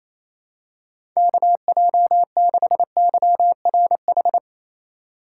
Morse code, heard as a keyed tone, about 28 wpm, 700 Hz. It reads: KJ6YRH
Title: USA Call Signs - 6 Characters - 3x Character Spacing - 28wpm